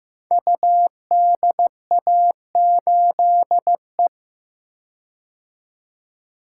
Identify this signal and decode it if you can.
Morse code, heard as a keyed tone, about 15 wpm, 700 Hz. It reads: UDA8E
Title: Code Groups - 5 Letter:Numbers - 15wpm